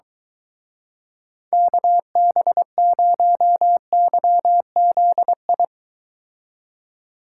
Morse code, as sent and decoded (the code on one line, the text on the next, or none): -.- -... ----- -.-- --.. ..
KB0YZI